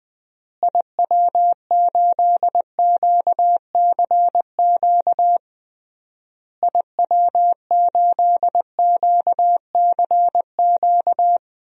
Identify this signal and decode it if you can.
Morse code, heard as a keyed tone, about 21 wpm, 700 Hz. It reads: IW8QCQ IW8QCQ
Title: DXCC-100 Call Signs - 6 Characters - Repeated 3x - 20wpm